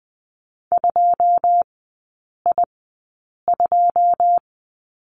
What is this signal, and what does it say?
Morse code, heard as a keyed tone, about 20 wpm, spaced as for 8 wpm, 700 Hz. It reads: 2I2
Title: 081 Lesson 14e - Number 2 - Three Characters 20wpm